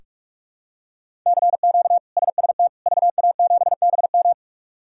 Morse code, heard as a keyed tone, about 41 wpm, 700 Hz. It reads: CQ SST VA7BM
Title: K1USN SST QSO Practice - 45wpm